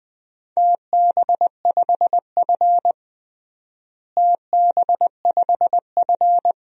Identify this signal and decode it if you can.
Morse code, heard as a keyed tone, about 21 wpm, 700 Hz. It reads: TB5F TB5F
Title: DXCC-100 Call Signs - 4 Characters - Repeated 3x - 20wpm